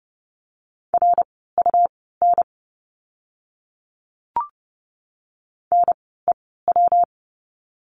Morse code, heard as a keyed tone, about 30 wpm, spaced as for 17 wpm, 700 Hz. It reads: RUN NEW
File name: course / 30:10wpm / 115 Lesson 19f - Letter w - Words 30wpm